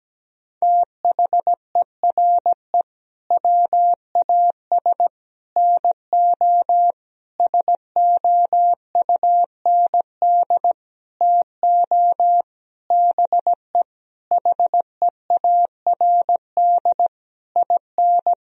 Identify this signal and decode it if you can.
Morse code, heard as a keyed tone, about 17 wpm, 700 Hz. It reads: THERE WAS NO SOUND TO BE HEARD IN
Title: Sentences from Top 1000 Words - Review Part 3 - 17wpm